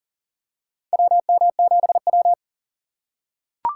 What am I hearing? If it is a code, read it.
Morse code, heard as a keyed tone, about 40 wpm, 700 Hz. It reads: WM7W
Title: USA Call Signs - 4 Characters - 3x Character Spacing - 40wpm